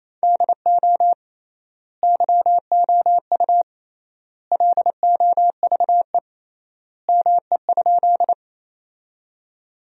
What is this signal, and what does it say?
Morse code, heard as a keyed tone, about 28 wpm, 700 Hz. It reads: DO YOU LOVE ME?
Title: 3x Word-Spacing - Sentences from Top 400 Words - 28wpm